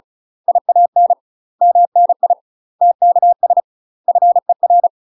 Morse code, heard as a keyed tone, about 34 wpm, 700 Hz. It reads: IAN MNI TKS FER
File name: UK MASTER QSOs TRAINING / 35WPM / UK MASTER QSOs 0003